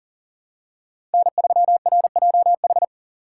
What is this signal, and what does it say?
Morse code, heard as a keyed tone, about 38 wpm, 700 Hz. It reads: N3RJH
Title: USA Call Signs - 5 Characters - 2x Character Spacing - 40wpm